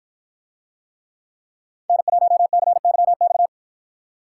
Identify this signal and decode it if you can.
Morse code, heard as a keyed tone, about 53 wpm, 700 Hz. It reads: N1CYX